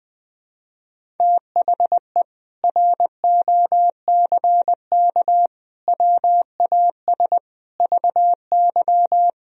Morse code, heard as a keyed tone, about 20 wpm, 700 Hz. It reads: THE ROCK WAS VY
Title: Sentences from Top 400 Words and Abbreviations 20wpm